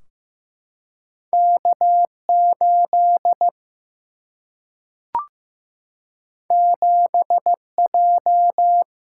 Morse code, heard as a keyed tone, about 15 wpm, 700 Hz. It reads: K8 7J
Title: Single-Letter Single-Number 15wpm